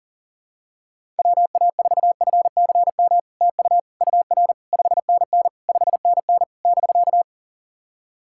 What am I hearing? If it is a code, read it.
Morse code, heard as a keyed tone, about 40 wpm, 700 Hz. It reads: WA4FCM TU UR 5NN 5NN <BK>
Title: SOTA QSO Practice 40wpm